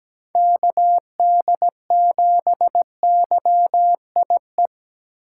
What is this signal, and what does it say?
Morse code, heard as a keyed tone, about 17 wpm, 700 Hz. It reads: KD7YIE
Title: USA Call Signs - 3x Character Spacing - 17wpm